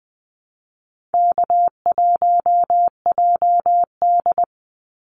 Morse code, heard as a keyed tone, about 20 wpm, 700 Hz. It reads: K1JD
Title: NA SOTA Activator Callsigns 20wpm